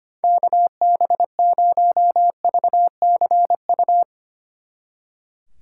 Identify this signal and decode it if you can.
Morse code, heard as a keyed tone, about 30 wpm, 700 Hz. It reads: KB0VCU
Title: Speed Racing - USA 2x3 Call Signs 25wpm